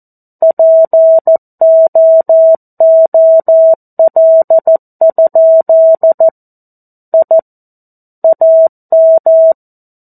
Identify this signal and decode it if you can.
Morse code, heard as a keyed tone, about 14 wpm, 650 Hz. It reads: POOL? I AM
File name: 02-Chapter-02 14 WPM